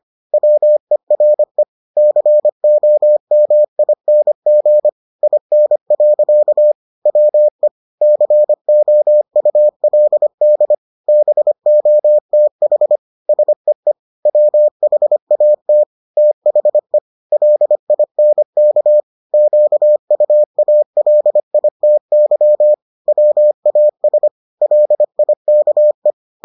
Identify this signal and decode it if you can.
Morse code, heard as a keyed tone, about 25 wpm, 600 Hz. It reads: WERE COMING IN. WE COULD BOTH SEE WHAT THE LINK QUALITY WAS LIKE